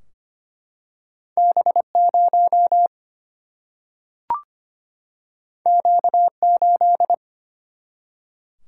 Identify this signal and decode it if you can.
Morse code, heard as a keyed tone, about 25 wpm, 700 Hz. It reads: B0 Q8
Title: Single-Letter Single-Number 25wpm